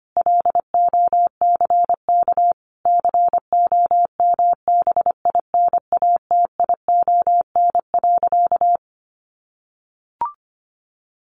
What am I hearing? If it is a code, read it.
Morse code, heard as a keyed tone, about 25 wpm, 700 Hz. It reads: LOCK COMBINATION.
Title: Sentences from Top 1300 Words 25wpm